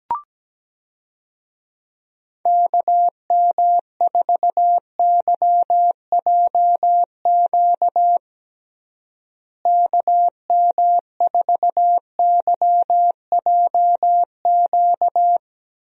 Morse code, heard as a keyed tone, about 17 wpm, 700 Hz. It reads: KM4YJQ KM4YJQ
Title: USA Call Signs - 6 Characters - Repeated 3x - 17wpm